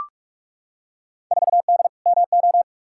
Morse code, heard as a keyed tone, about 42 wpm, 700 Hz. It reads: 4D MO
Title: Field Day Exchanges Encore - 45wpm